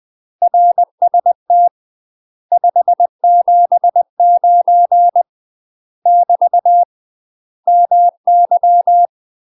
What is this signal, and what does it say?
Morse code, heard as a keyed tone, about 20 wpm, 700 Hz. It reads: RST 579 <BT> MY
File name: UK MASTER QSOs TRAINING / 20WPM / UK MASTER QSOs 0025